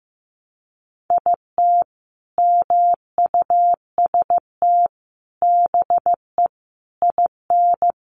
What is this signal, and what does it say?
Morse code, heard as a keyed tone, about 15 wpm, 700 Hz. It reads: IT MUST BE IN
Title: Sentences from Top 300 Words 15wpm